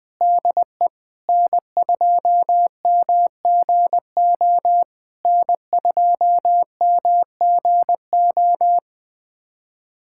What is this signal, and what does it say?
Morse code, heard as a keyed tone, about 20 wpm, 700 Hz. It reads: DE N2MGO N2MGO
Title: SOTA QSO Practice 20wpm